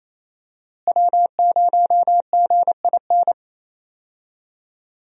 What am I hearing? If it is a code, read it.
Morse code, heard as a keyed tone, about 28 wpm, 700 Hz. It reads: W0GIN
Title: USA Call Signs - 5 Characters - Repeated 2x - 28wpm